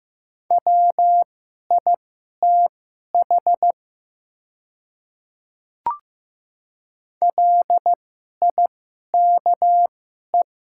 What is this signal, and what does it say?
Morse code, heard as a keyed tone, about 15 wpm, spaced as for 11 wpm, 700 Hz. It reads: WITH LIKE
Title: 2x Character Spacing - Top 100 Words - 15wpm